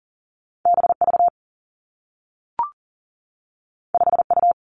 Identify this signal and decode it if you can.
Morse code, heard as a keyed tone, about 39 wpm, 700 Hz. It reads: B V 5 U
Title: Two Challenging Letter-Number Combinations 40wpm